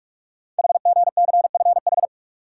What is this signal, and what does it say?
Morse code, heard as a keyed tone, about 44 wpm, 700 Hz. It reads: HGCFH
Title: Code Groups - 5 Letters - 45wpm